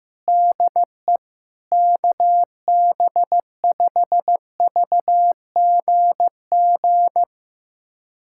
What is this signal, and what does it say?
Morse code, heard as a keyed tone, about 15 wpm, 700 Hz. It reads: DE KB5VGG
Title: USA Callsign DE Callsign - 6 Characters - 15wpm